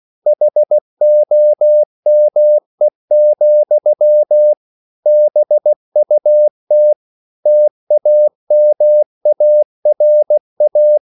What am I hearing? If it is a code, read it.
Morse code, heard as a keyed tone, about 16 wpm, 600 Hz. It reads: HOME, BUT TAMARA